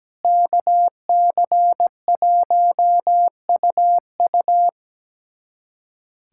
Morse code, heard as a keyed tone, about 17 wpm, 700 Hz. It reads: KC1UU
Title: USA Callsign DE Callsign - 5 Characters - 17wpm